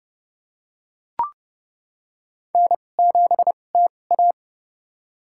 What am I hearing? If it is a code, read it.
Morse code, heard as a keyed tone, about 29 wpm, spaced as for 21 wpm, 690 Hz. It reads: N7TA
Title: USA Call Signs - 4 Characters - 2x Character Spacing - 30wpm